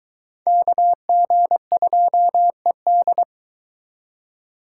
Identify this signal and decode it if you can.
Morse code, heard as a keyed tone, about 23 wpm, 700 Hz. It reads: KG2ED